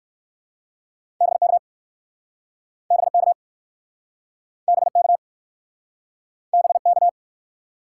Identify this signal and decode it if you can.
Morse code, heard as a keyed tone, about 53 wpm, 700 Hz. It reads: BK BK BK BK
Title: Speed-Racing Top 90 QSO Elements 45wpm